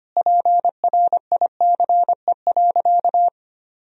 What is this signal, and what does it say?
Morse code, heard as a keyed tone, about 25 wpm, 700 Hz. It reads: PRICE.
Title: Sentences from Top 1000 Words - Review Part 4 - 25wpm